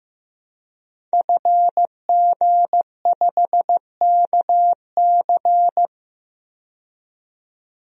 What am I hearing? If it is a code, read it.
Morse code, heard as a keyed tone, about 15 wpm, 700 Hz. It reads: FG5KC
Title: DXCC-100 Call Signs - 5 Characters - 15wpm